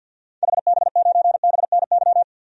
Morse code, heard as a keyed tone, about 53 wpm, 700 Hz. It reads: HB9BNY
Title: DXCC-100 Call Signs - 6 Characters - 50wpm